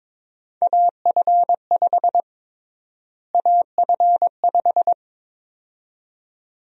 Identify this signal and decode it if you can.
Morse code, heard as a keyed tone, about 22 wpm, 700 Hz. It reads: AF5 AF5
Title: USA Call Sign Prefixes - Repeated 2x - 22wpm